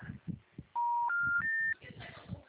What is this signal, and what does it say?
Special information tone.
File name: out- -2012- - - .69384